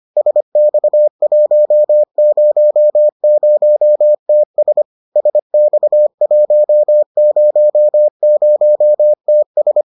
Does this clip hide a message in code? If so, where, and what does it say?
Yes: Morse code from 0.2 to 9.8 s, heard as a keyed tone, about 25 wpm, 600 Hz. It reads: SX100TS SX100TS